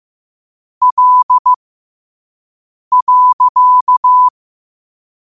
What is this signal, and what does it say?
Morse code, heard as a keyed tone, about 15 wpm, spaced as for 5 wpm, 1000 Hz. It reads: L.
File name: 5wpm